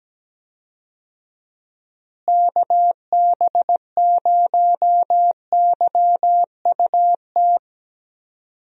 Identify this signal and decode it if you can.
Morse code, heard as a keyed tone, about 17 wpm, 700 Hz. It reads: KB0YUT
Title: USA Call Signs - 6 Characters - Repeated 2x - 17wpm